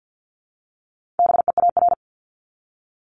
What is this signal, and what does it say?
Morse code, heard as a keyed tone, about 48 wpm, 700 Hz. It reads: BEAR
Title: ING Suffix - Top 100 Words with Base Words 50wpm